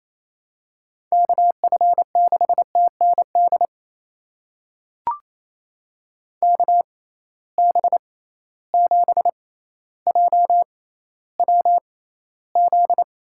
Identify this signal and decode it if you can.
Morse code, heard as a keyed tone, about 28 wpm, spaced as for 14 wpm, 700 Hz. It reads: KF6TND KB7JWZ
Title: USA Call Signs - 6 Characters - 6x Character Spacing - 28wpm